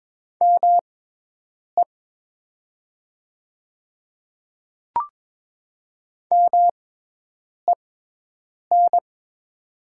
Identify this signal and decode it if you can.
Morse code, heard as a keyed tone, about 22 wpm, spaced as for 8 wpm, 700 Hz. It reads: ME MEN
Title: 6x Character Spacing - Top 500 Words - 22wpm